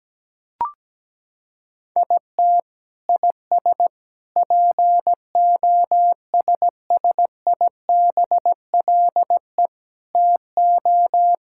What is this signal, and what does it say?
Morse code, heard as a keyed tone, about 17 wpm, 700 Hz. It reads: IT IS POSSIBLE TO
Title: Sentences from Top 1000 Words - Review Part 6 - 17wpm